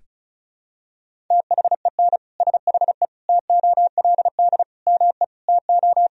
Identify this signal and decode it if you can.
Morse code, heard as a keyed tone, about 36 wpm, 700 Hz. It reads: THEN SHE TOLD ME TO
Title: Sentences from Top 400 Words and Abbreviations 35wpm